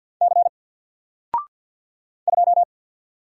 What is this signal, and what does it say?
Morse code, heard as a keyed tone, about 48 wpm, 700 Hz. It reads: C 2
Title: Single Letter-Number Alt 50wpm